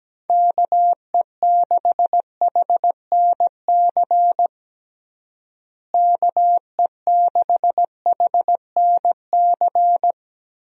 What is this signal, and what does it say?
Morse code, heard as a keyed tone, about 17 wpm, 700 Hz. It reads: KE6HNC KE6HNC
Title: USA Call Signs - 6 Characters - Repeated 2x - 17wpm